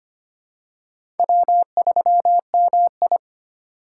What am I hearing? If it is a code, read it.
Morse code, heard as a keyed tone, about 25 wpm, 700 Hz. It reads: W3MI
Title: USA Call Signs - 3x Character Spacing - 25wpm